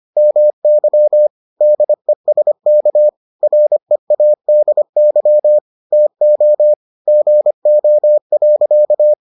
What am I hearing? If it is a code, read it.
Morse code, heard as a keyed tone, about 25 wpm, 600 Hz. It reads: MY DESK READY TO GO.